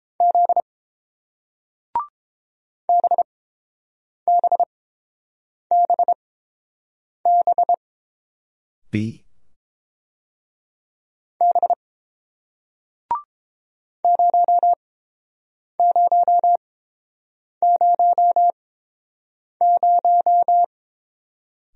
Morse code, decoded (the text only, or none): Z BBBB B 0000